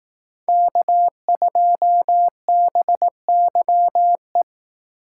Morse code, heard as a keyed tone, about 19 wpm, 700 Hz. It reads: K2BYE
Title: Speed Racing - USA 1x3 Call Signs 15wpm